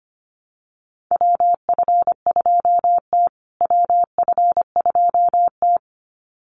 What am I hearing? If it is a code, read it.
Morse code, heard as a keyed tone, about 25 wpm, 700 Hz. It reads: WF2T WF2T
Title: USA Call Signs Repeated 2x 25wpm